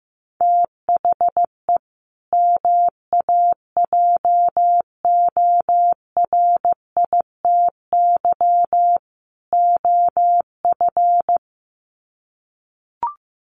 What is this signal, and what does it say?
Morse code, heard as a keyed tone, about 15 wpm, 700 Hz. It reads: THE MAJORITY OF
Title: Sets of 3 Words 15wpm